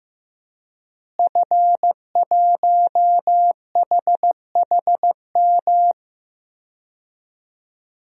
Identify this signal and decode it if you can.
Morse code, heard as a keyed tone, about 15 wpm, 700 Hz. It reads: F1HHM
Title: DXCC-100 Call Signs - 5 Characters - Repeated 2x - 15wpm